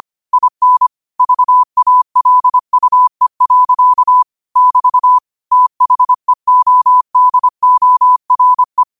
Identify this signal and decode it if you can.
Morse code, heard as a keyed tone, about 25 wpm, 1000 Hz. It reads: IN VALUE. <BT> THEODORE